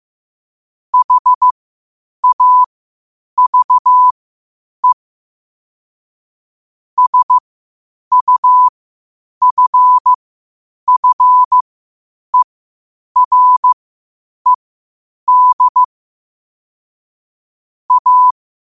Morse code, heard as a keyed tone, about 15 wpm, spaced as for 8 wpm, 1000 Hz. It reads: HAVE SUFFERED A
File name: qotdfiles / 8wpm